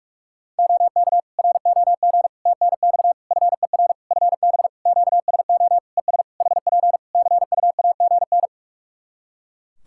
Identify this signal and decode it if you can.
Morse code, heard as a keyed tone, about 41 wpm, 700 Hz. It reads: OK ROG TNX FER FB QSO ES HP CUAGN
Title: Practice QSO - Exchange 4 - Ending - 45wpm